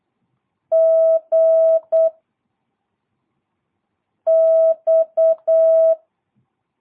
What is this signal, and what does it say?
Morse code, heard as a keyed tone, about 8 wpm, 650 Hz. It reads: G X
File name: jh8cbh / 2022cw